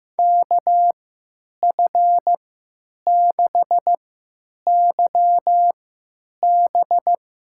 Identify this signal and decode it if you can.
Morse code, heard as a keyed tone, about 15 wpm, spaced as for 9 wpm, 700 Hz. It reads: KF6YB
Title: USA Call Signs - 3x Character Spacing - 15wpm